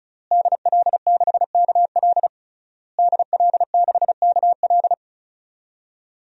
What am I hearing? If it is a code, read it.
Morse code, heard as a keyed tone, about 36 wpm, 700 Hz. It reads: DL6KL DL6KL
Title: DXCC-100 Call Signs - 5 Characters - Repeated 3x - 35wpm